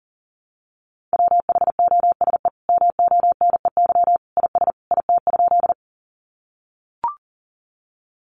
Morse code, heard as a keyed tone, about 40 wpm, 700 Hz. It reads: WHOSE MONEY IS IT?
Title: Sentences from Top 1200 Words 40wpm